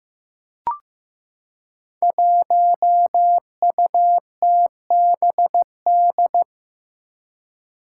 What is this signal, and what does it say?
Morse code, heard as a keyed tone, about 15 wpm, 700 Hz. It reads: 1UTBD
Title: Five Challenging Letter-Number Combinations 15wpm